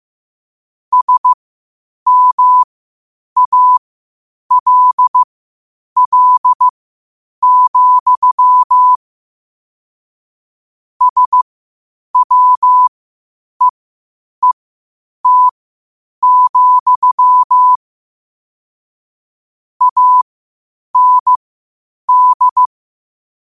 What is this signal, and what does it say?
Morse code, heard as a keyed tone, about 15 wpm, spaced as for 8 wpm, 1000 Hz. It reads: SMALL, SWEET, AND
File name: qotdfiles / 8wpm